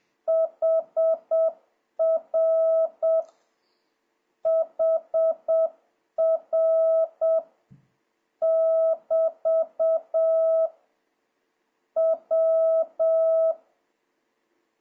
Morse code, heard as a keyed tone, about 7 wpm, 1300 Hz. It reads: HR HR <BT> W
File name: jh8cbh / 2026cw